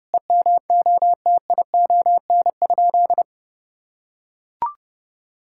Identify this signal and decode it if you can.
Morse code, heard as a keyed tone, about 32 wpm, 700 Hz. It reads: EMOTION?
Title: Sentences from Top 1600 Words 30wpm